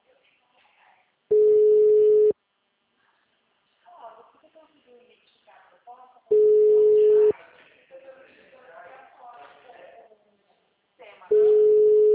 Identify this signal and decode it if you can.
Ringback tone (Europe, 425 Hz), with speech.
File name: out- -2012- - - .41834